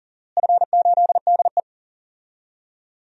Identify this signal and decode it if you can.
Morse code, heard as a keyed tone, about 40 wpm, 700 Hz. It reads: F8DE
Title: DXCC-100 Call Signs - 4 Characters - Repeated 3x - 40wpm